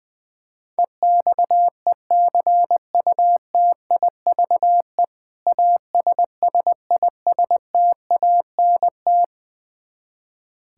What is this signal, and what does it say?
Morse code, heard as a keyed tone, about 20 wpm, 700 Hz. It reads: EXECUTIVE ASSISTANT
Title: Practice QSO - Exchange 3 - More Info - 20wpm